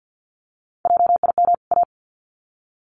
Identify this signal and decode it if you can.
Morse code, heard as a keyed tone, about 48 wpm, 700 Hz. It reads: JIM A
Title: CWT Exchange 50wpm